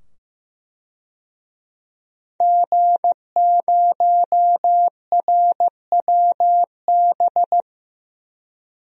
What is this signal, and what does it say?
Morse code, heard as a keyed tone, about 15 wpm, 700 Hz. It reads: G0RWB
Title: DXCC-100 Call Signs - 5 Characters - 15wpm